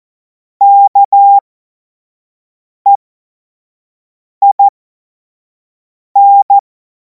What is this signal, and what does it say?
Morse code, heard as a keyed tone, about 14 wpm, spaced as for 5 wpm, 800 Hz. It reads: KEIN
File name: Morsekurs / Witze / Witz 25. August 2026 25 F70